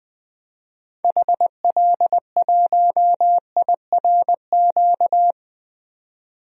Morse code, heard as a keyed tone, about 21 wpm, 700 Hz. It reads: HL1IRQ
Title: DXCC-100 Call Signs - 6 Characters - 20wpm